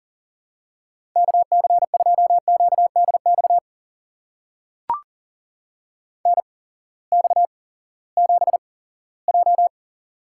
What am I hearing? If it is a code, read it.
Morse code, heard as a keyed tone, about 40 wpm, spaced as for 20 wpm, 700 Hz. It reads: KC2QDX NX7J
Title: USA Call Signs - 8x Character Spacing - 40wpm